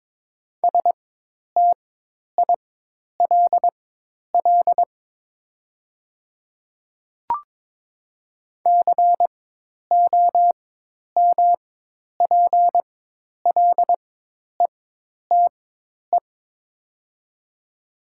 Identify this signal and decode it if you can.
Morse code, heard as a keyed tone, about 22 wpm, spaced as for 10 wpm, 700 Hz. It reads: STILL COMPLETE
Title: 4x Character Spacing - Top 500 Words - 22wpm